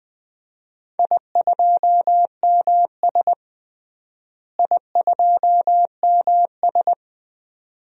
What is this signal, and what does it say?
Morse code, heard as a keyed tone, about 19 wpm, 700 Hz. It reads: I2MS I2MS
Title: DXCC-100 Call Signs - 4 Characters - Repeated 2x - 20wpm